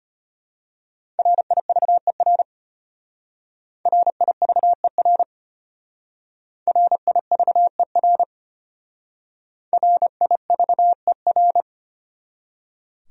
Morse code, heard as a keyed tone, about 32 wpm, 700 Hz. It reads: RIVER RIVER RIVER RIVER